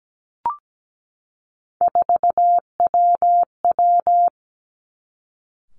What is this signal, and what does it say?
Morse code, heard as a keyed tone, about 17 wpm, 700 Hz. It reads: 4WW
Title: Three Challenging Letter-Number Combinations 17wpm